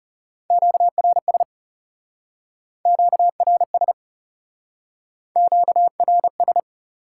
Morse code, heard as a keyed tone, about 38 wpm, 700 Hz. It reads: QRS QRS QRS